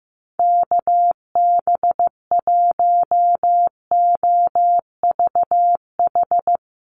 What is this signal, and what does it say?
Morse code, heard as a keyed tone, about 15 wpm, 700 Hz. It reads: KB1OVH
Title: USA Call Signs Repeated 2x 15wpm